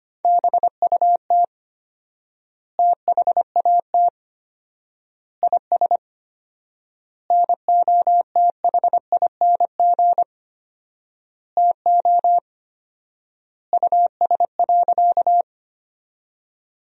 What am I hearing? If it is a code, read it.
Morse code, heard as a keyed tone, about 25 wpm, 700 Hz. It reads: BUT THAT IS NOTHING TO US.
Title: 4x Word-Spacing - Sentences from Top 500 Words - 25wpm